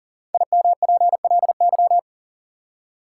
Morse code, heard as a keyed tone, about 40 wpm, 700 Hz. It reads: IMPLY
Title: Five Letter Words - 40wpm